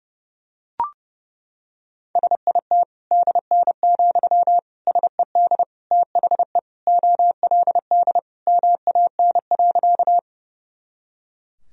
Morse code, heard as a keyed tone, about 30 wpm, 700 Hz. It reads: SIT DN, SED THE OLD MAN.
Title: Sentences from Top 500 Words and Abbreviations 30wpm